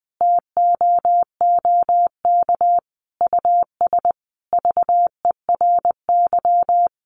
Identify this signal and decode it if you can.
Morse code, heard as a keyed tone, about 20 wpm, 700 Hz. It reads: TOOK US VERY
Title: Sentences from Top 1300 Words 20wpm